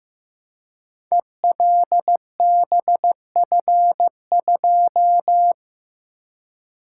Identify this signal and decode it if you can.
Morse code, heard as a keyed tone, about 15 wpm, 700 Hz. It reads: ELBF2
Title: Code Groups - 5 Letter:Numbers - 15wpm